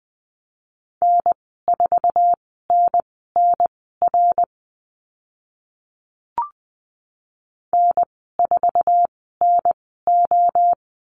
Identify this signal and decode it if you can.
Morse code, heard as a keyed tone, about 20 wpm, spaced as for 15 wpm, 700 Hz. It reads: N4NNR N4NO
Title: 062 Lesson 11g - Letter h - Call Signs 20wpm